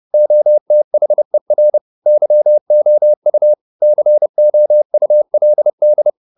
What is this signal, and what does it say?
Morse code, heard as a keyed tone, about 30 wpm, 600 Hz. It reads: OTHER YOU COULD